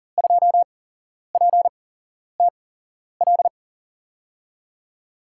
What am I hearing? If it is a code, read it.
Morse code, heard as a keyed tone, about 40 wpm, spaced as for 11 wpm, 700 Hz. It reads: 2PTL
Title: USA Call Signs - 5 Characters - 8x Character Spacing - 40wpm